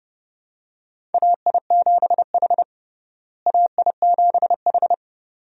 Morse code, heard as a keyed tone, about 30 wpm, 700 Hz. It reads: AI7H AI7H
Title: USA Call Signs - 4 Characters - Repeated 2x - 30wpm